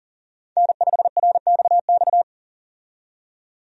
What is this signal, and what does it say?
Morse code, heard as a keyed tone, about 40 wpm, 700 Hz. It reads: N5RXX
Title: USA Call Signs - 5 Characters - 40wpm